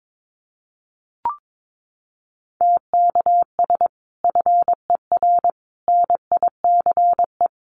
Morse code, heard as a keyed tone, about 22 wpm, 700 Hz. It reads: TKS FER NICE
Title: Common QSO Phrases 22wpm